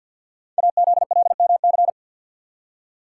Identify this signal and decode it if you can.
Morse code, heard as a keyed tone, about 52 wpm, 700 Hz. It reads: AZLMC
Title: Code Groups - 5 Letters - 50wpm